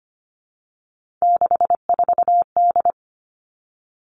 Morse code, heard as a keyed tone, about 25 wpm, 700 Hz. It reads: 64D
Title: Three Challenging Letter-Number Combinations 25wpm